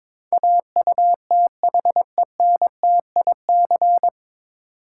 Morse code, heard as a keyed tone, about 21 wpm, 700 Hz. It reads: AUTHENTIC
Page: Morse Code Ninja practice 22wpm